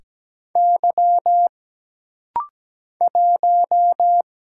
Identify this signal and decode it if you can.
Morse code, heard as a keyed tone, about 17 wpm, 700 Hz. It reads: Y 1
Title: Single Letter-Number Alt - Rapid-Fire 17wpm